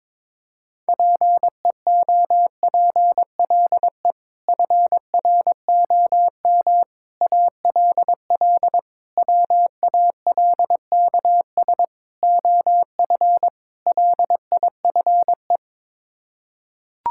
Morse code, heard as a keyed tone, about 22 wpm, 700 Hz. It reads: PEOPLE FROM ALL WALKS OF LIFE
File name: practice / 22wpm / Sets of 6 Words - 22wpm